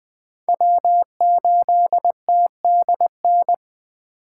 Morse code, heard as a keyed tone, about 21 wpm, 700 Hz. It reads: W8TDN
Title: Speed Racing - USA 1x3 Call Signs 17wpm